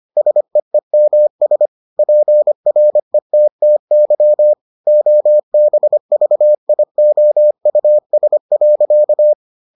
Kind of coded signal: Morse code

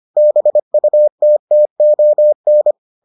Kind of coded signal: Morse code

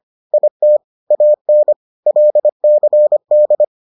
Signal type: Morse code

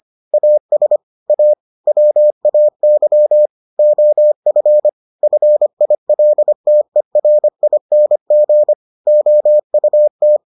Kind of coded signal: Morse code

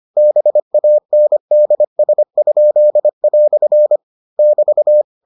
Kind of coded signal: Morse code

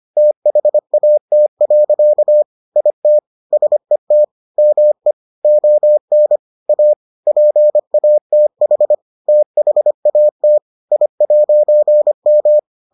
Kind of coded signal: Morse code

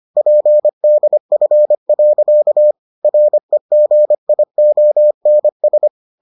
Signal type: Morse code

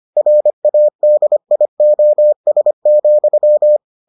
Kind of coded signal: Morse code

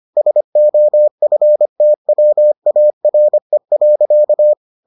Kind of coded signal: Morse code

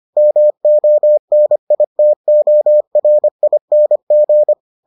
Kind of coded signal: Morse code

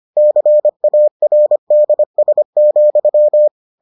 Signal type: Morse code

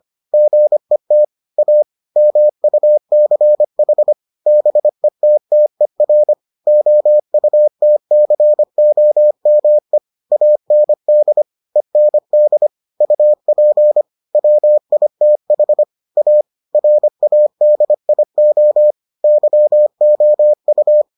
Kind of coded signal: Morse code

